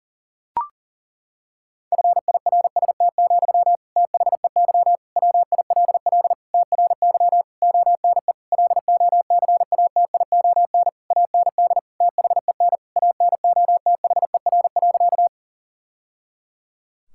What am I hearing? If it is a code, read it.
Morse code, heard as a keyed tone, about 40 wpm, 700 Hz. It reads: FIRST, THEY WILL TRY ONE LOCATION AND THEN ANOTHER.
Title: Sentences from Top 1000 Words - Review Part 6 - 40wpm